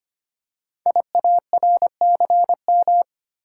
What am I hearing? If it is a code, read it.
Morse code, heard as a keyed tone, about 25 wpm, 690 Hz. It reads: IARCM
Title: Code Groups - 5 Letters - 25wpm